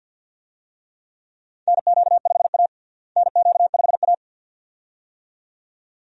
Morse code, heard as a keyed tone, about 47 wpm, 700 Hz. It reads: NQ5A NQ5A
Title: USA Call Signs - 4 Characters - Repeated 2x - 50wpm